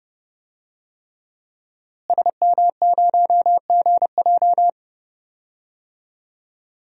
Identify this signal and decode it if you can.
Morse code, heard as a keyed tone, about 30 wpm, 700 Hz. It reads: SM0GJ
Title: DXCC-100 Call Signs - 5 Characters - Repeated 3x - 30wpm